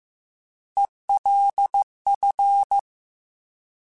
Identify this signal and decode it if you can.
Morse code, heard as a keyed tone, about 15 wpm, 800 Hz. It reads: ELF